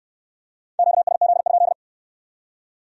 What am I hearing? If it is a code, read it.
Morse code, heard as a keyed tone, about 63 wpm, 700 Hz. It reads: QSZ?